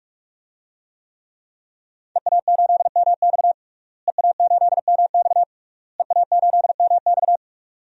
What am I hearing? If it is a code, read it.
Morse code, heard as a keyed tone, about 46 wpm, 700 Hz. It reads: EA8MX EA8MX EA8MX
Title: DXCC-100 Call Signs - 5 Characters - Repeated 3x - 45wpm